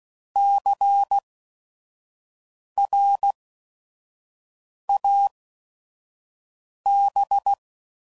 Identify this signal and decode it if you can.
Morse code, heard as a keyed tone, about 16 wpm, spaced as for 5 wpm, 800 Hz. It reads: CRAB
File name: Prefixes / By First Letter / Cs